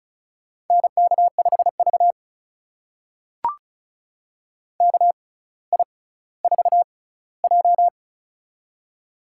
Morse code, heard as a keyed tone, about 36 wpm, spaced as for 18 wpm, 700 Hz. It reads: NK5V KI4J
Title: USA Call Signs - 4 Characters - 6x Character Spacing - 35wpm